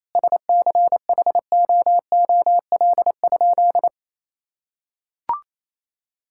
Morse code, heard as a keyed tone, about 28 wpm, 700 Hz. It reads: SCHOOL?
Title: Sentences from Top 1000 Words - Review Part 5 - 28wpm